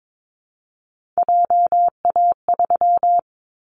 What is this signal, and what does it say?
Morse code, heard as a keyed tone, about 22 wpm, 700 Hz. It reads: JA3
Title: Callsign Prefixes for Top 25 DXCC - 22wpm